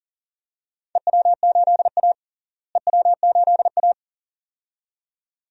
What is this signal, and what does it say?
Morse code, heard as a keyed tone, about 38 wpm, 700 Hz. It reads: EW8A EW8A
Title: DXCC-100 Call Signs - 4 Characters - Repeated 2x - 40wpm